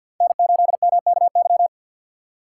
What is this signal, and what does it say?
Morse code, heard as a keyed tone, about 62 wpm, 700 Hz. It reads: N8MKY
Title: Speed Racing - USA 1x3 Call Signs 50wpm